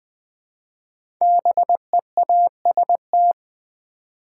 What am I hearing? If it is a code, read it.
Morse code, heard as a keyed tone, about 20 wpm, 700 Hz. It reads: BEAST
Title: 5 Letter Words - First 3 Letters Given - 20wpm